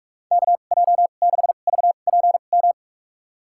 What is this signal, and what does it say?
Morse code, heard as a keyed tone, about 46 wpm, spaced as for 33 wpm, 700 Hz. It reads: KJ6VPM
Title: USA Call Signs - 2x Character Spacing - 45wpm